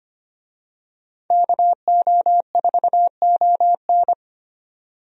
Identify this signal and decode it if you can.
Morse code, heard as a keyed tone, about 25 wpm, 700 Hz. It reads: KO4ON
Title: USA Call Signs - 5 Characters - 25wpm